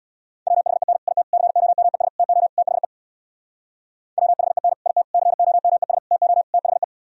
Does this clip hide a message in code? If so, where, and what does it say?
Yes: Morse code from 0.5 to 6.8 s, heard as a keyed tone, about 65 wpm, 700 Hz. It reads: WHEN IT COMES TO THE WHEN IT COMES TO THE